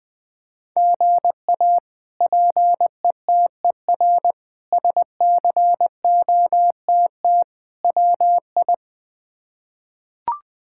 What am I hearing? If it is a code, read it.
Morse code, heard as a keyed tone, about 20 wpm, 700 Hz. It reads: GA PETER SCOTT WI
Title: K1USN SST QSO Practice - 20wpm